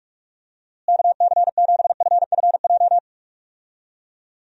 Morse code, heard as a keyed tone, about 42 wpm, 700 Hz. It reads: KC7FFJ
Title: USA Call Signs - 6 Characters - Repeated 2x - 45wpm